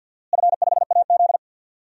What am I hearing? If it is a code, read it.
Morse code, heard as a keyed tone, about 46 wpm, 700 Hz. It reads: F5AZ
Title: DXCC-100 Call Signs - 4 Characters - Repeated 2x - 50wpm